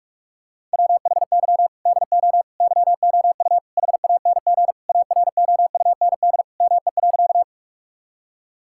Morse code, heard as a keyed tone, about 42 wpm, 700 Hz. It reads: WHY DO YOU HANG AROUND ME.
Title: Sentences from Top 1600 Words 45wpm